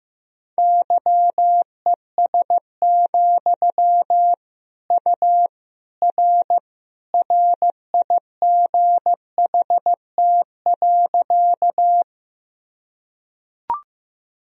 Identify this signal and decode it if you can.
Morse code, heard as a keyed tone, about 15 wpm, 700 Hz. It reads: YES, U R RIGHT.
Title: Sentences from Top 500 Words and Abbreviations 15wpm